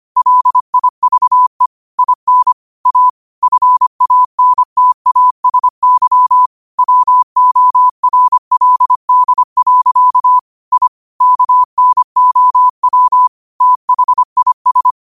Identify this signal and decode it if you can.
Morse code, heard as a keyed tone, about 25 wpm, 1000 Hz. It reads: LIVE IN A FANTASY WORLD. I KNOW THIS
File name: qotdfiles / 25wpm